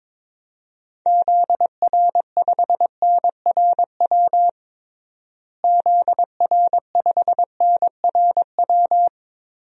Morse code, heard as a keyed tone, about 22 wpm, 700 Hz. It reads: ZR5NRW ZR5NRW
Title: DXCC-100 Call Signs - 6 Characters - Repeated 3x - 22wpm